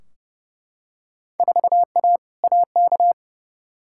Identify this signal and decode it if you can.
Morse code, heard as a keyed tone, about 30 wpm, 700 Hz. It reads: 4A AK
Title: Field Day Exchanges - 30wpm